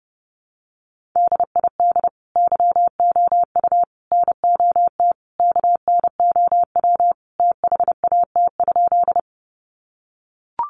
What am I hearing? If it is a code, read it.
Morse code, heard as a keyed tone, about 30 wpm, 700 Hz. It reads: DID YOU NOT KNOW THAT?
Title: Sentences from Top 600 Words 30wpm